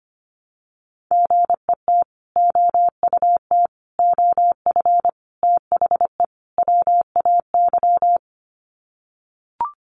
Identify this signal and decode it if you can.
Morse code, heard as a keyed tone, about 25 wpm, 690 Hz. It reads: GET OUT OF THE WAY
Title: Sets of 5 Words Encore 25wpm